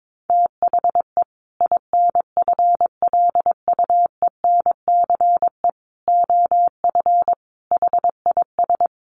Morse code, heard as a keyed tone, about 22 wpm, 700 Hz. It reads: THE INFLUENCE OF HIS
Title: Sentences from Top 1100 Words 22wpm